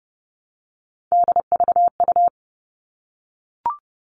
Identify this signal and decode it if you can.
Morse code, heard as a keyed tone, about 30 wpm, 700 Hz. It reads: DVU
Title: All Cut Numbers - 3-Digit 30wpm